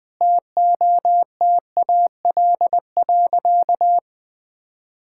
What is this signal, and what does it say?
Morse code, heard as a keyed tone, about 20 wpm, 700 Hz. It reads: TOTAL.
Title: Sentences with 8-Word Phrases - 20wpm